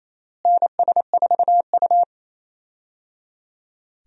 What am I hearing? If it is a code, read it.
Morse code, heard as a keyed tone, about 28 wpm, 700 Hz. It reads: NS4U
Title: USA Call Signs - 4 Characters - Repeated 3x - 28wpm